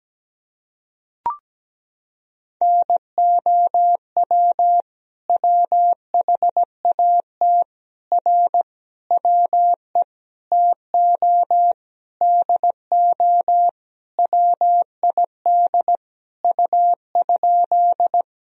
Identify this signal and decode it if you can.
Morse code, heard as a keyed tone, about 17 wpm, 700 Hz. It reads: NOW WHAT R WE TO DO WID U?
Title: Sentences from Top 100 Words and Abbreviations 17wpm